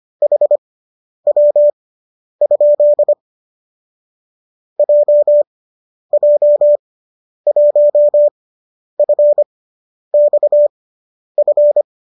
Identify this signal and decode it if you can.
Morse code, heard as a keyed tone, about 25 wpm, spaced as for 10 wpm, 590 Hz. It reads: HW? JJ1FXF
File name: qso005 10